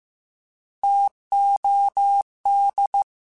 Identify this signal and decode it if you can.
Morse code, heard as a keyed tone, about 15 wpm, 800 Hz. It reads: TOD
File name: Alphabetical / Alphabetical mp3s / Ts @13